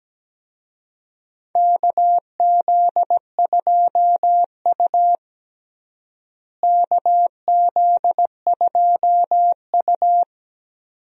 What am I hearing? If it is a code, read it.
Morse code, heard as a keyed tone, about 17 wpm, 700 Hz. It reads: KZ2U KZ2U
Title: USA Call Signs - 4 Characters - Repeated 2x - 17wpm